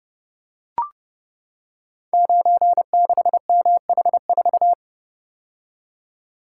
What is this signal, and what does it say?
Morse code, heard as a keyed tone, about 30 wpm, 700 Hz. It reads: 96MH4
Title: Code Groups - 5 Letter:Numbers - 30wpm